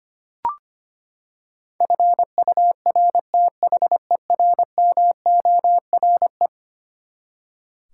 Morse code, heard as a keyed tone, about 25 wpm, 700 Hz. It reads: FURTHERMORE
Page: morse-code-ninja-practice-25wpm